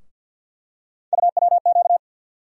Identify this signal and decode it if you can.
Morse code, heard as a keyed tone, about 53 wpm, 700 Hz. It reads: UWQ
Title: USA Call Sign Postfixes - 3 Letters 50wpm